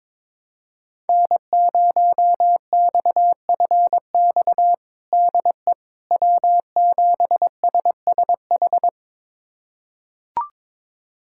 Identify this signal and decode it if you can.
Morse code, heard as a keyed tone, about 23 wpm, 700 Hz. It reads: N0XFX DE W7SSH
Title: USA Callsign DE Callsign - 5 Characters - 22wpm